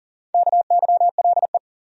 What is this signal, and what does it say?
Morse code, heard as a keyed tone, about 39 wpm, 700 Hz. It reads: KYLE
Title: USA Common Names - Max 5 Letters - 40wpm